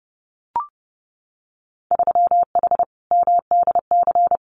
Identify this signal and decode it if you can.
Morse code, heard as a keyed tone, about 30 wpm, 700 Hz. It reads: 3H MDC
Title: Winter Field Day Exchanges Encore 30wpm